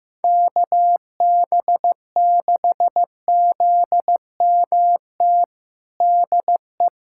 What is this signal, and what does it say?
Morse code, heard as a keyed tone, about 15 wpm, 700 Hz. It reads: KB6ZMT DE
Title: USA Callsign DE Callsign 15wpm